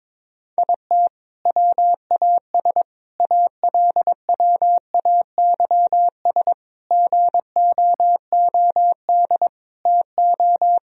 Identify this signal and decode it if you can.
Morse code, heard as a keyed tone, about 22 wpm, 700 Hz. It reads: IT WAS ALWAYS GOOD TO
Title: Sentences from Top 1000 Words - Review Part 4 - 22wpm